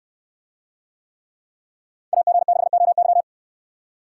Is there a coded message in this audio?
Yes: Morse code from 2.1 to 3.2 s, heard as a keyed tone, about 69 wpm, 700 Hz. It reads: AG6OY